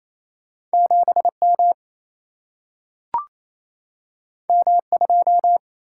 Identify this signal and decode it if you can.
Morse code, heard as a keyed tone, about 28 wpm, 700 Hz. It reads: 7M M2
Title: Single-Letter Single-Number 28wpm